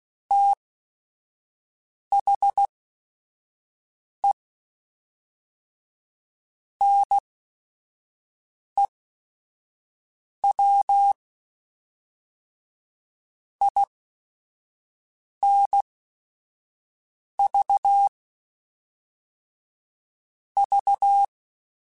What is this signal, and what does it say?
Morse code, heard as a keyed tone, about 16 wpm, spaced as for 5 wpm, 800 Hz. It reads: THE NEW INV V
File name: QSOs / QSOs-mp3s / WD0FBT 1@5